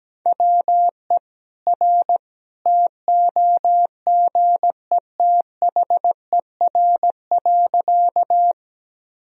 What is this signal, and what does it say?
Morse code, heard as a keyed tone, about 17 wpm, 700 Hz. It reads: WE R TOGETHER.
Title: Sentences from Top 300 Words and Abbreviations 17wpm